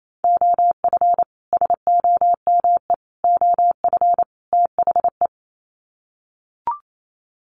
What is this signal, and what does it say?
Morse code, heard as a keyed tone, about 28 wpm, 700 Hz. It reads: OF SOME OF THE
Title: Sets of 4 Words 28wpm